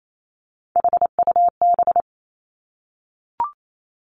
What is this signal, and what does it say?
Morse code, heard as a keyed tone, about 28 wpm, 700 Hz. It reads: HUB
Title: Three Challenging Letter-Number Combinations 28wpm